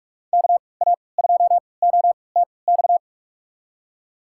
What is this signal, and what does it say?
Morse code, heard as a keyed tone, about 43 wpm, spaced as for 25 wpm, 700 Hz. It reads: KA2OTX
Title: USA Call Signs - 6 Characters - 3x Character Spacing - 45wpm